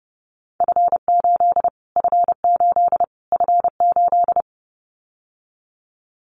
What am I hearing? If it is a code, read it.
Morse code, heard as a keyed tone, about 30 wpm, 700 Hz. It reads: F8 F8 F8
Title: Callsign Prefixes for Top 25 DXCC - Repeated 3x - 30wpm